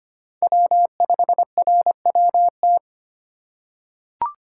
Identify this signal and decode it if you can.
Morse code, heard as a keyed tone, about 25 wpm, 700 Hz. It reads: W5RWT
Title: USA Call Signs - 4x Character Spacing - 25wpm